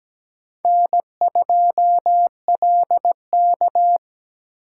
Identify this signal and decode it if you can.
Morse code, heard as a keyed tone, about 17 wpm, 700 Hz. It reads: N2LK
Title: K1USN SST QSO Practice - 17wpm